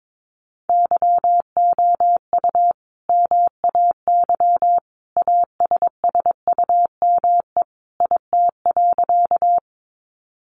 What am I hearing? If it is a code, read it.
Morse code, heard as a keyed tone, about 22 wpm, 700 Hz. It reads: YOU MAY ASSUME IT.
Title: Sentences from Top 1500 Words 22wpm